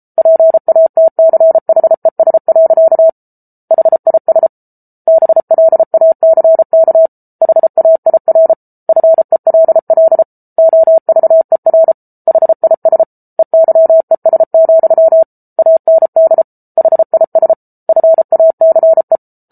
Morse code, heard as a keyed tone, about 34 wpm, 650 Hz. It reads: PATCHES. HIS BLACK HAIR FELL OVER HIS EYES, AND HIS FACE